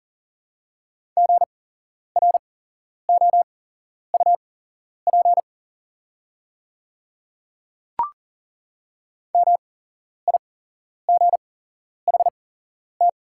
Morse code, heard as a keyed tone, about 40 wpm, spaced as for 11 wpm, 700 Hz. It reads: GROUP MIGHT
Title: 8x Character Spacing - Top 500 Words - 40wpm